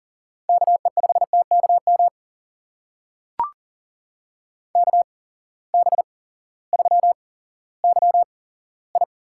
Morse code, heard as a keyed tone, about 40 wpm, spaced as for 18 wpm, 700 Hz. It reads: KE5TKM KB3YI
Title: USA Call Signs - 6 Characters - 8x Character Spacing - 40wpm